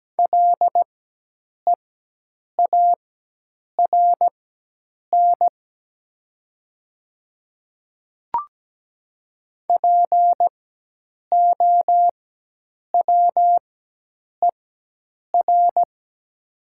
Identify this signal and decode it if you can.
Morse code, heard as a keyed tone, about 17 wpm, spaced as for 8 wpm, 700 Hz. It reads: LEARN POWER
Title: 4x Character Spacing - Top 500 Words - 17wpm